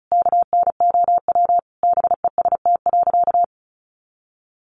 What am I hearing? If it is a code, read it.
Morse code, heard as a keyed tone, about 34 wpm, 700 Hz. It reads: KNOW BEST.
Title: Sentences from Top 600 Words 35wpm